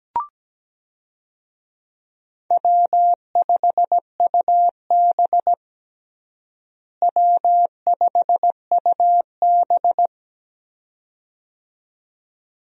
Morse code, heard as a keyed tone, about 17 wpm, 700 Hz. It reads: W5UB W5UB
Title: USA Call Signs - 4 Characters - Repeated 2x - 17wpm